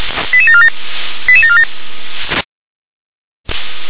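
Five-tone selective call (ZVEI), heard as quick runs of five tones, tones of about 70 ms. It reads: 88747 88747
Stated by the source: ZVEI five-tone call received by SDR.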